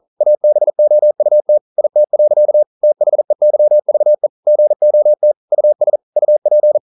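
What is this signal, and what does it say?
Morse code, heard as a keyed tone, about 40 wpm, 600 Hz. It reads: ABOUT IT. THEYVE GOT US UP
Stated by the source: Morse code practice at 41 wpm, from CW med Gnister.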